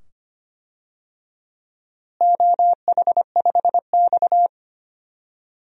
Morse code, heard as a keyed tone, about 25 wpm, 700 Hz. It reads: OH5X
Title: DXCC-100 Call Signs - 4 Characters - Repeated 2x - 25wpm